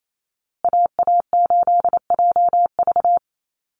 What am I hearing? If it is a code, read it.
Morse code, heard as a keyed tone, about 28 wpm, 700 Hz. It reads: AA8JV
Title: USA Call Signs Repeated 2x 28wpm